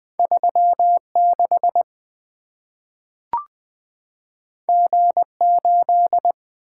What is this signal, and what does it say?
Morse code, heard as a keyed tone, about 20 wpm, 700 Hz. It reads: 3 6 G 8
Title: Two Letter-Number Combinations - 20wpm